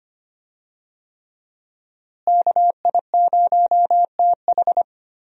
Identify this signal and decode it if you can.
Morse code, heard as a keyed tone, about 25 wpm, 700 Hz. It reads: KI0TH
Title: USA Call Signs - 5 Characters - Repeated 3x - 25wpm